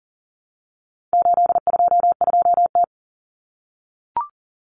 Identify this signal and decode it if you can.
Morse code, heard as a keyed tone, about 40 wpm, 700 Hz. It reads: 822T
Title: Common Cut Numbers - 4-Digit 40wpm